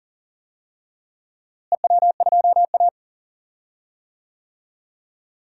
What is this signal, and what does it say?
Morse code, heard as a keyed tone, about 40 wpm, 700 Hz. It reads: EW2A
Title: DXCC-100 Call Signs - 4 Characters - Repeated 2x - 40wpm